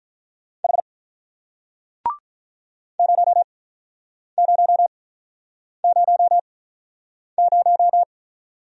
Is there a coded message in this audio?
Yes: Morse code from 0.6 to 8.0 s, heard as a keyed tone, about 46 wpm, spaced as for 9 wpm, 700 Hz. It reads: H 0000